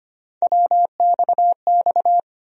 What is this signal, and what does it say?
Morse code, heard as a keyed tone, about 25 wpm, 700 Hz. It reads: WXX
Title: USA Call Sign Postfixes - 3 Letters 25wpm